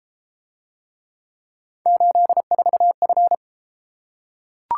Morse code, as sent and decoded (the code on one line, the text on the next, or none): ---.. ....- ..-.
84F